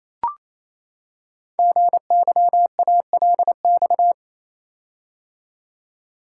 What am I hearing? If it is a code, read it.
Morse code, heard as a keyed tone, about 27 wpm, 700 Hz. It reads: GYALX
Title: Code Groups - 5 Letters - 28wpm